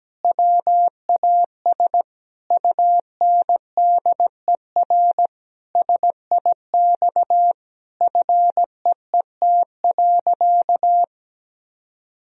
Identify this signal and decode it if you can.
Morse code, heard as a keyed tone, about 17 wpm, 700 Hz. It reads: WAS UNDER SIX FEET.
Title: Sentences from Top 1000 Words - Review Part 2 - 17wpm